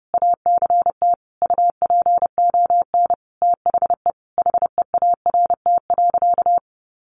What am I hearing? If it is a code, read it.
Morse code, heard as a keyed tone, about 30 wpm, 700 Hz. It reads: ACT UPON THE HEART.
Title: Sentences from Top 1200 Words 30wpm